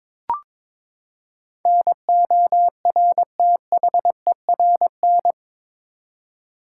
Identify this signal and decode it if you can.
Morse code, heard as a keyed tone, about 22 wpm, 700 Hz. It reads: NORTHERN
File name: practice / 22wpm / Eight Letter Words - 22wpm